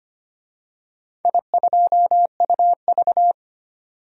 Morse code, heard as a keyed tone, about 25 wpm, 700 Hz. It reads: I2UV
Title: DXCC-100 Call Signs - 4 Characters - 25wpm